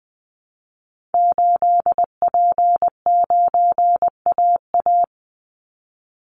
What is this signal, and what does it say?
Morse code, heard as a keyed tone, about 20 wpm, 700 Hz. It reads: 8P9AA
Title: Call Signs Repeated 3x - 20wpm